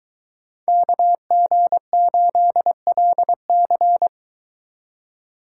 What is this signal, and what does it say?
Morse code, heard as a keyed tone, about 22 wpm, 700 Hz. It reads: KG8LC